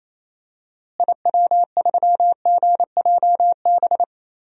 Morse code, heard as a keyed tone, about 28 wpm, 700 Hz. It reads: IW3GJB
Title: DXCC-100 Call Signs - 6 Characters - Repeated 2x - 28wpm